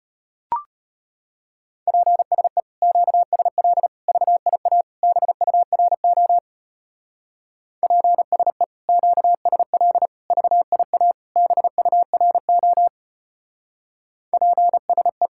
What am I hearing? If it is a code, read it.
Morse code, heard as a keyed tone, about 32 wpm, 700 Hz. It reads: PSE QSL VIA BURO PSE QSL VIA BURO PSE